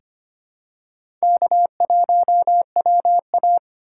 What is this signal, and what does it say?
Morse code, heard as a keyed tone, about 25 wpm, 700 Hz. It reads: K1WA
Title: USA Call Signs - 3x Character Spacing - 25wpm